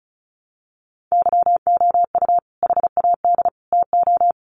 Morse code, heard as a keyed tone, about 34 wpm, 700 Hz. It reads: YOU HAD TO
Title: Sentences from Top 600 Words 35wpm